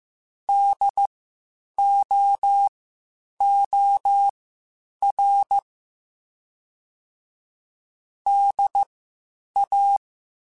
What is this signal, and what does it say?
Morse code, heard as a keyed tone, about 15 wpm, spaced as for 8 wpm, 800 Hz. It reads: DOOR DA
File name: Spaced Alphas / Spaced Alphas mp3s / Ds sp @13